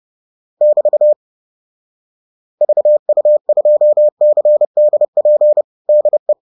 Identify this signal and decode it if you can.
Morse code, heard as a keyed tone, about 30 wpm, 600 Hz. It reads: <BT> VU2CDP DE
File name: 30wpm-7